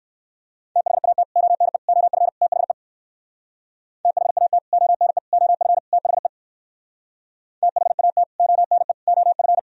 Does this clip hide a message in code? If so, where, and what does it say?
Yes: Morse code from 0.8 to 9.6 s, heard as a keyed tone, about 56 wpm, 700 Hz. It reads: THAT ONE OF THE THAT ONE OF THE THAT ONE OF